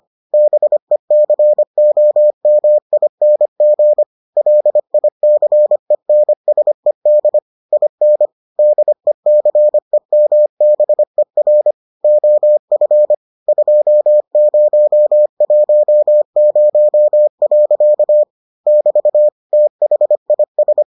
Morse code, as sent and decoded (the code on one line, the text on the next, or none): -... -.-. --- -- .. -. --. / .-.. .. -.-. -. ... -.. / .. -. / -.. -.-. -- -... .-. / --- ..-. / ..--- ----- .---- ----- .-.-.- / -...- / - .... .. ...
BECOMING LICENSED IN DECEMBER OF 2010. <BT> THIS